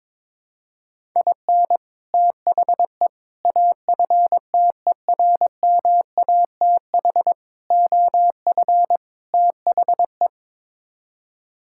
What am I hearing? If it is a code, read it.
Morse code, heard as a keyed tone, about 23 wpm, 700 Hz. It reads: IN THE AFTERMATH OF THE
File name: practice / 22wpm / Build A Phrase - 5-Word Phrases - 22wpm